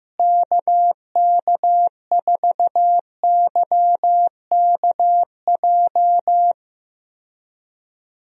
Morse code, heard as a keyed tone, about 15 wpm, 700 Hz. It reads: KK4YKJ
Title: USA Call Signs - 6 Characters - Repeated 2x - 15wpm